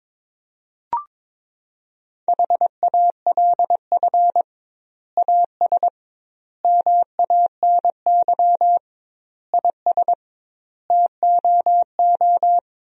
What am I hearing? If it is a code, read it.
Morse code, heard as a keyed tone, about 22 wpm, 700 Hz. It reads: HALF AS MANY IS TOO
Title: 2x Word-Spacing - Sentences from Top 400 Words - 22wpm